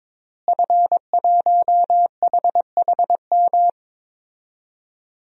Morse code, heard as a keyed tone, about 22 wpm, 700 Hz. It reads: F1HHM
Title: DXCC-100 Call Signs - 5 Characters - Repeated 2x - 22wpm